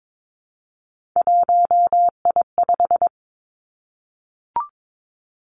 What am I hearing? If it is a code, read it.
Morse code, heard as a keyed tone, about 22 wpm, 700 Hz. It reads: 1I5
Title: Three Challenging Letter-Number Combinations 22wpm